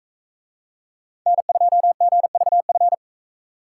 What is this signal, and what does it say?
Morse code, heard as a keyed tone, about 42 wpm, 700 Hz. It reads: N2GVF